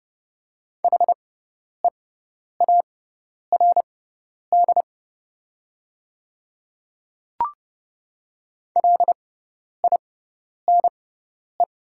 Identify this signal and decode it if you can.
Morse code, heard as a keyed tone, about 30 wpm, spaced as for 10 wpm, 700 Hz. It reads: HEARD LINE
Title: 6x Character Spacing - Top 500 Words - 30wpm